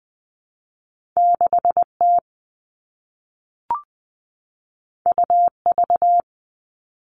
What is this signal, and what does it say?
Morse code, heard as a keyed tone, about 20 wpm, 700 Hz. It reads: 6 T U V
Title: All Cut Numbers - 2-Digit 20wpm